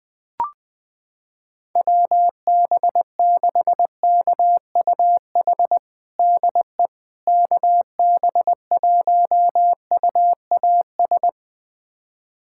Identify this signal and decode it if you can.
Morse code, heard as a keyed tone, about 20 wpm, 700 Hz. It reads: WB6KUH DE KB1UAS
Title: USA Callsign DE Callsign - 6 Characters - 20wpm